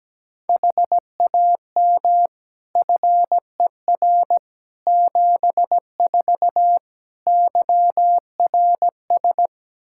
Morse code, heard as a keyed tone, about 17 wpm, 700 Hz. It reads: HAM FER 74 YRS
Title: Practice QSO - Exchange 3 - More Info - 17wpm